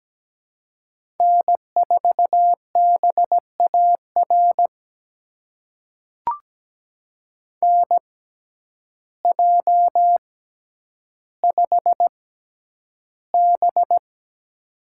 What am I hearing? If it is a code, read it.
Morse code, heard as a keyed tone, about 17 wpm, spaced as for 9 wpm, 700 Hz. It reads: N4BAR NJ5B
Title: USA Call Signs - 6x Character Spacing - 17wpm